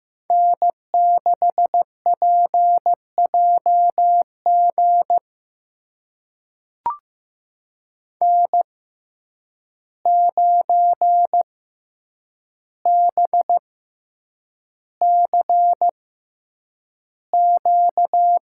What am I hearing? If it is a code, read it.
Morse code, heard as a keyed tone, about 15 wpm, spaced as for 8 wpm, 700 Hz. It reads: N6PJG N9BCQ
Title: USA Call Signs - 5 Characters - 6x Character Spacing - 15wpm